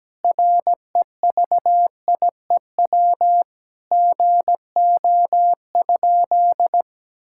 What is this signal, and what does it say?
Morse code, heard as a keyed tone, about 17 wpm, 700 Hz. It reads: REVIEW GO?
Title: Sentences from Top 1000 Words - Review Part 6 - 17wpm